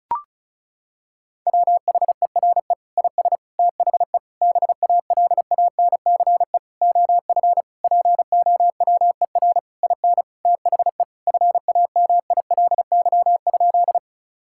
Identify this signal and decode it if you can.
Morse code, heard as a keyed tone, about 34 wpm, 700 Hz. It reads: WHERE IS THE BALANCE OF POWER IN THE FAMILY?
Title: Sentences from Top 1000 Words - Review Part 6 - 35wpm